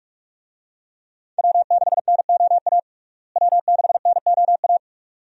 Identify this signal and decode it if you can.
Morse code, heard as a keyed tone, about 46 wpm, 700 Hz. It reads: W6NOA W6NOA
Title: USA Call Signs - 5 Characters - Repeated 2x - 45wpm